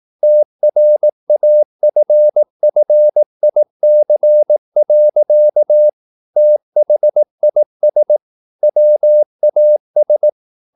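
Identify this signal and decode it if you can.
Morse code, heard as a keyed tone, about 18 wpm, 600 Hz. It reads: TRAFFIC. THIS WAS
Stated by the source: Morse practice at 18 wpm, English text, from CW med Gnister.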